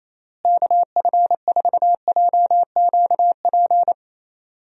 Morse code, heard as a keyed tone, about 27 wpm, 700 Hz. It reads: KF4JQP
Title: USA Call Signs - 6 Characters - Repeated 2x - 28wpm